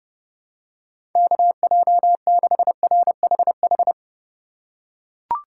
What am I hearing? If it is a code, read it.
Morse code, heard as a keyed tone, about 30 wpm, 700 Hz. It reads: KJ6RHH
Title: USA Call Signs - 6 Characters - 2x Character Spacing - 30wpm